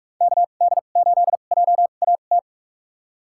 Morse code, heard as a keyed tone, about 41 wpm, spaced as for 31 wpm, 700 Hz. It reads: KD8JAT
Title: USA Call Signs - 2x Character Spacing - 45wpm